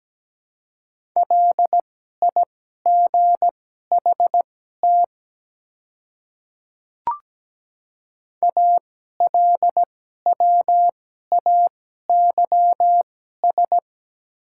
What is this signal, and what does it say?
Morse code, heard as a keyed tone, about 17 wpm, spaced as for 12 wpm, 700 Hz. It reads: LIGHT ALWAYS
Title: 2x Character Spacing - Top 300 Words - 17wpm